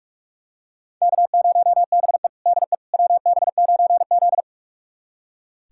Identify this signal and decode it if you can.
Morse code, heard as a keyed tone, about 44 wpm, 700 Hz. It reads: K0BE DE WB9Z
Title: USA Callsign DE Callsign - 4 Characters - 45wpm